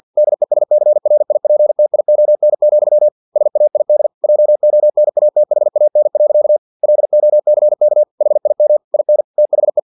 Morse code, heard as a keyed tone, about 50 wpm, 600 Hz. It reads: DESCRIPTION, SAID JONATHAN. LOCK HIM IN THE
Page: Never 49wpm